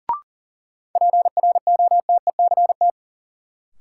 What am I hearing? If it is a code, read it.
Morse code, heard as a keyed tone, about 37 wpm, 700 Hz. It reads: PROTECT
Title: Sentences from Top 900 Words - New Word Drill - 40wpm